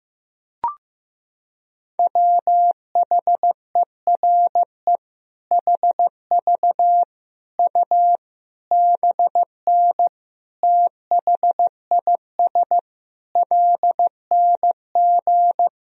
Morse code, heard as a keyed tone, about 16 wpm, 700 Hz. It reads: WHERE HV U BN THIS LNG
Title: Sentences from Top 300 Words and Abbreviations 15wpm